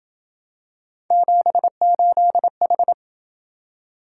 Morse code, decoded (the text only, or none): 78H